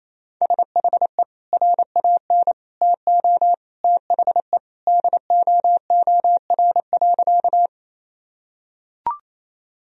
Morse code, heard as a keyed tone, about 28 wpm, 700 Hz. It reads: SHE RAN TO THE DOOR.
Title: Sentences from Top 1000 Words - Review Part 3 - 28wpm